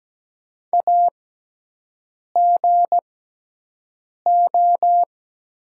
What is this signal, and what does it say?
Morse code, heard as a keyed tone, about 17 wpm, spaced as for 6 wpm, 700 Hz. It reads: AGO
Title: 6x Character Spacing - Top 500 Words - 17wpm